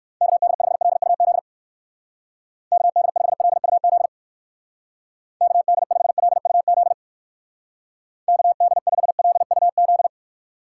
Morse code, heard as a keyed tone, about 59 wpm, 700 Hz. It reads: KD5LUZ KD5LUZ KD5LUZ KD5LUZ